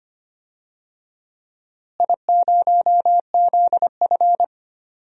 Morse code, heard as a keyed tone, about 25 wpm, 700 Hz. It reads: I0ZF
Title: DXCC-100 Call Signs - 4 Characters - Repeated 2x - 25wpm